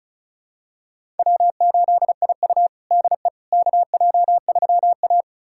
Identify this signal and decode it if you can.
Morse code, heard as a keyed tone, about 34 wpm, 700 Hz. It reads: W8IU DE KJ3A
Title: USA Callsign DE Callsign - 4 Characters - 35wpm